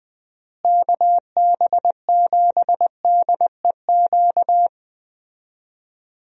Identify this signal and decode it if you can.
Morse code, heard as a keyed tone, about 20 wpm, 700 Hz. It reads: KB7DEQ
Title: USA Call Signs - 6 Characters - Repeated 2x - 20wpm